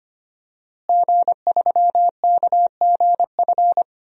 Morse code, heard as a keyed tone, about 25 wpm, 700 Hz. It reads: G3KGF
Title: DXCC-100 Call Signs - 5 Characters - Repeated 2x - 25wpm